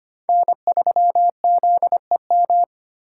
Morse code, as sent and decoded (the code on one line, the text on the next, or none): -. ...-- --.. --
N3ZEM